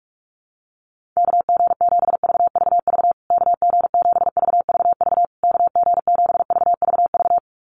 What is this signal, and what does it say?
Morse code, heard as a keyed tone, about 46 wpm, 700 Hz. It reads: KG7VVV KG7VVV KG7VVV
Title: USA Call Signs Repeated 3x 45wpm